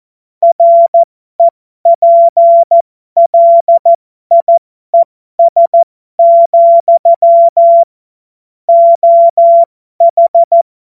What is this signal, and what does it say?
Morse code, heard as a keyed tone, about 14 wpm, spaced as for 12 wpm, 680 Hz. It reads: REPLIES, OH
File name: Morsekurs / Witze